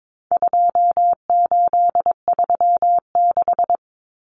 Morse code, heard as a keyed tone, about 22 wpm, 700 Hz. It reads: 2836
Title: Common Cut Numbers - 4-Digit 22wpm